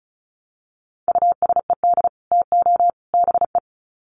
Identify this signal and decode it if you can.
Morse code, heard as a keyed tone, about 34 wpm, 700 Hz. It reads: USED TO BE
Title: Sets of 3 Words 35wpm